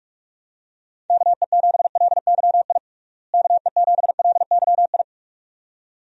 Morse code, heard as a keyed tone, about 46 wpm, 700 Hz. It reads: KE7LYI KE7LYI
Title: USA Call Signs - 6 Characters - Repeated 2x - 45wpm